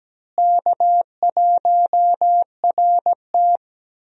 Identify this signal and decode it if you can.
Morse code, heard as a keyed tone, about 17 wpm, 700 Hz. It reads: K1RT
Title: USA Call Signs - 4 Characters - Repeated 3x - 17wpm